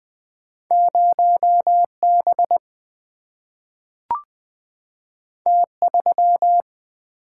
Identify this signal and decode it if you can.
Morse code, heard as a keyed tone, about 20 wpm, 700 Hz. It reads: 0B T3
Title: Single-Letter Single-Number 20wpm